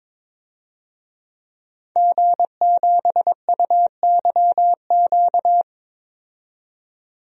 Morse code, heard as a keyed tone, about 22 wpm, 700 Hz. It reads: G7UYQ
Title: DXCC-100 Call Signs - 5 Characters - 22wpm